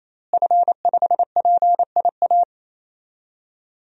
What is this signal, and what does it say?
Morse code, heard as a keyed tone, about 28 wpm, 700 Hz. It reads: F5PIA
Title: DXCC-100 Call Signs - 5 Characters - Repeated 3x - 28wpm